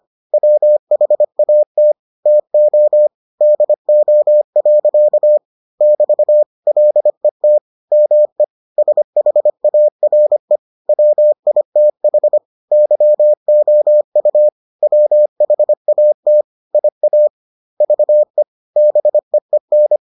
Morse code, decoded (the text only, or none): WHAT TO DO. <BT> LET ME SHARE WITH YOU WHAT IA VE BEEN